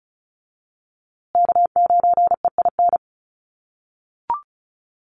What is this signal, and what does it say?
Morse code, heard as a keyed tone, about 34 wpm, 700 Hz. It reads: K9EIN
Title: USA Call Signs Repeated 2x 35wpm